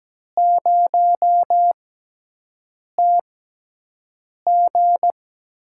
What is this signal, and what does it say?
Morse code, heard as a keyed tone, about 17 wpm, spaced as for 6 wpm, 700 Hz. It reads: 0TG
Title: USA Call Signs - 5 Characters - 6x Character Spacing - 17wpm